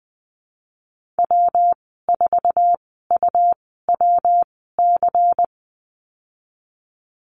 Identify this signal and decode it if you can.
Morse code, heard as a keyed tone, about 20 wpm, spaced as for 15 wpm, 700 Hz. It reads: W4UWC
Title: 122 Lesson 20g - Number 3 - Call Signs 20wpm